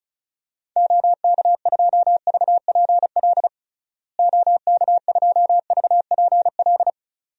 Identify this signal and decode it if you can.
Morse code, heard as a keyed tone, about 34 wpm, 700 Hz. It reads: OK2VPL OK2VPL
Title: DXCC-100 Call Signs - 6 Characters - Repeated 2x - 35wpm